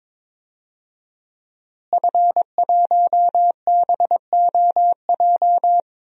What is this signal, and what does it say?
Morse code, heard as a keyed tone, about 22 wpm, 700 Hz. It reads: F1BOJ
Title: DXCC-100 Call Signs - 5 Characters - Repeated 3x - 22wpm